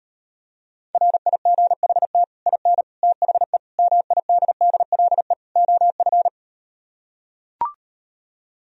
Morse code, heard as a keyed tone, about 38 wpm, 700 Hz. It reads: RIGHT IN THE MIDDLE OF